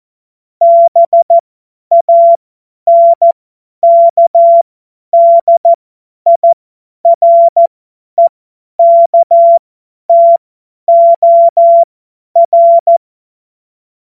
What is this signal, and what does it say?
Morse code, heard as a keyed tone, about 14 wpm, spaced as for 10 wpm, 680 Hz. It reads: BANKDIREKTOR
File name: Morsekurs / Witze / Witz 23. Mai 2023 50 F70